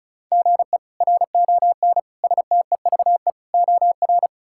Morse code, heard as a keyed tone, about 36 wpm, 700 Hz. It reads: GE RON STEVE OR
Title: K1USN SST QSO Practice - 35wpm